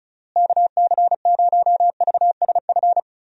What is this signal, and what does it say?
Morse code, heard as a keyed tone, about 34 wpm, 700 Hz. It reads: KC0VSF
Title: USA Call Signs - 6 Characters - Repeated 2x - 35wpm